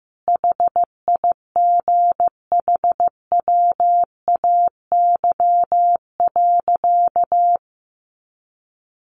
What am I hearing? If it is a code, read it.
Morse code, heard as a keyed tone, about 15 wpm, 700 Hz. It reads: HIGHWAY.
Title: Sentences from Top 1400 Words 15wpm